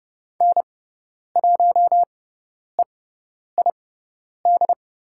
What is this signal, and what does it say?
Morse code, heard as a keyed tone, about 30 wpm, spaced as for 10 wpm, 700 Hz. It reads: N1EID
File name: Lesson 18g - Letter m - Call Signs 30wpm